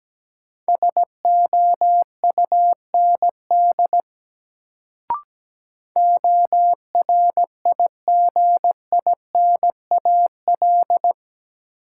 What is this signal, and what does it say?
Morse code, heard as a keyed tone, about 17 wpm, 700 Hz. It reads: SOUND ORIGINAL
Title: Sentences from Top 900 Words - New Word Drill - 17wpm